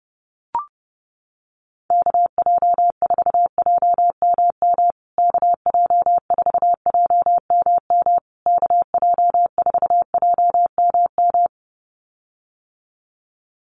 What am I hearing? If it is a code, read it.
Morse code, heard as a keyed tone, about 30 wpm, 700 Hz. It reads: KJ4JMM KJ4JMM KJ4JMM
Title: USA Call Signs Repeated 3x 30wpm